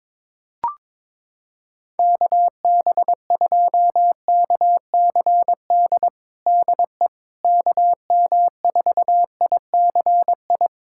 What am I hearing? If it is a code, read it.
Morse code, heard as a keyed tone, about 22 wpm, 700 Hz. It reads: KB2KCD DE KM4ICI
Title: USA Callsign DE Callsign 22wpm